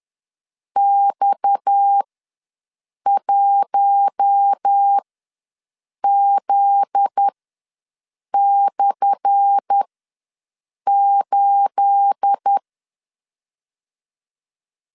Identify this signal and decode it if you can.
Morse code, heard as a keyed tone, about 10 wpm, spaced as for 6 wpm, 800 Hz. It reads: X1Z/8
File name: LEZIONI / 10 SIGNS / 10A SLASH ws 10wpm 4121